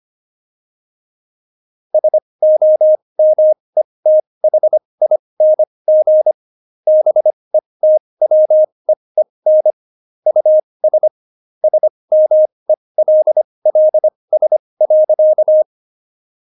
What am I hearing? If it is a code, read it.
Morse code, heard as a keyed tone, about 25 wpm, spaced as for 20 wpm, 620 Hz. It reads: SOMETHING BETWEEN US SMELLS.
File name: Jokes-25-char-20-eff